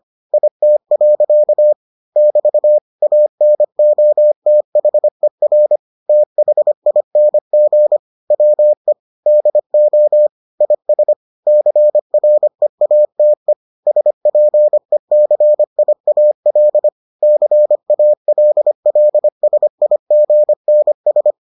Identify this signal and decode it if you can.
Morse code, heard as a keyed tone, about 25 wpm, 600 Hz. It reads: IT. <BT> ANOTHER THING WE DO IS CREATE SPECIAL CALLSIGNS